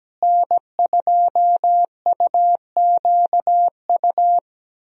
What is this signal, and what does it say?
Morse code, heard as a keyed tone, about 17 wpm, 700 Hz. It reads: N2UQU
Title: USA Call Signs - 5 Characters - Repeated 2x - 17wpm